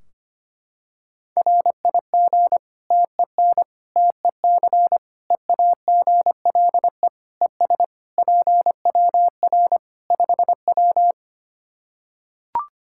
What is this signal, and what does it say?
Morse code, heard as a keyed tone, about 25 wpm, 700 Hz. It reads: RIG TEN TEC EAGLE ES PWR 5W
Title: Practice QSO - Exchange 2 - Info - 25wpm